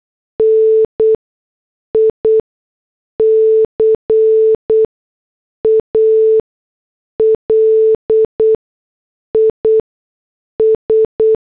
Morse code, heard as a keyed tone, about 8 wpm, spaced as for 6 wpm, 440 Hz. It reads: NICAL IS
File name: morse / headlines